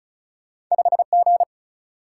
Morse code, heard as a keyed tone, about 36 wpm, 700 Hz. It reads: 5G
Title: Single-Letter Single-Number 35wpm